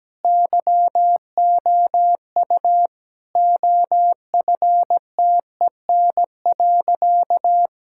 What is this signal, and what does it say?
Morse code, heard as a keyed tone, about 17 wpm, 700 Hz. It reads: YOU OFTEN.
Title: Sentences from Top 1000 Words - Review Part 6 - 17wpm